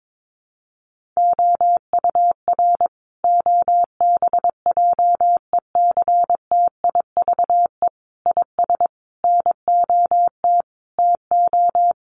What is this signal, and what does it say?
Morse code, heard as a keyed tone, about 22 wpm, 700 Hz. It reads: OUR OBJECTIVE IS NOT TO
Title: Sentences from Top 1200 Words 22wpm